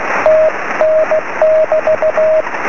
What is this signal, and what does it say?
Morse code, heard as a keyed tone, about 16 wpm, 620 Hz, with radio noise. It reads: TN<BT>
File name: sm7lcb / 20 23cm / 2219 dl4ebv 23cm